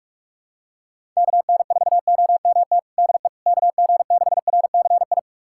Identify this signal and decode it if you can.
Morse code, heard as a keyed tone, about 45 wpm, 700 Hz. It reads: KN4OMT DE KG6RCI
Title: USA Callsign DE Callsign - 6 Characters - 45wpm